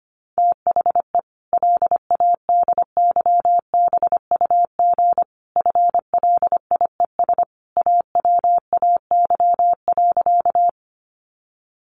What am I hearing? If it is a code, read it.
Morse code, heard as a keyed tone, about 25 wpm, 700 Hz. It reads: THE LADYBUG FLIES AWAY.
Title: Additional Easy Phrases 25wpm